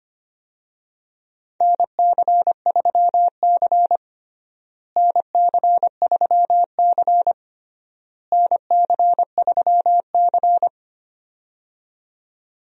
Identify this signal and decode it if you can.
Morse code, heard as a keyed tone, about 25 wpm, 700 Hz. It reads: NC3C NC3C NC3C
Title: USA Call Signs - 4 Characters - Repeated 3x - 25wpm